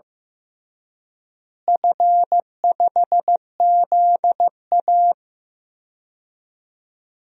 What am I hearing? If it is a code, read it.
Morse code, heard as a keyed tone, about 15 wpm, 700 Hz. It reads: F5ZA
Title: DXCC-100 Call Signs - 4 Characters - Repeated 2x - 15wpm